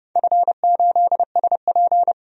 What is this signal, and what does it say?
Morse code, heard as a keyed tone, about 32 wpm, 700 Hz. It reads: F8SP
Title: DXCC-100 Call Signs - 4 Characters - 30wpm